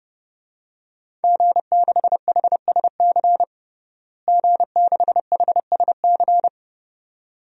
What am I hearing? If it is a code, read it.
Morse code, heard as a keyed tone, about 30 wpm, 700 Hz. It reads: G6HSC G6HSC
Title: DXCC-100 Call Signs - 5 Characters - Repeated 2x - 30wpm